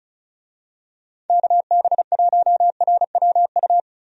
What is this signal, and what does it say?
Morse code, heard as a keyed tone, about 34 wpm, 700 Hz. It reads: KB1RWU
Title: USA Call Signs - 6 Characters - 35wpm